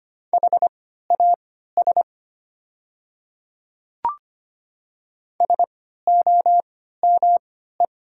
Morse code, heard as a keyed tone, about 25 wpm, spaced as for 14 wpm, 700 Hz. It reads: HAS SOME
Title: 3x Character Spacing - Top 200 Words - 25wpm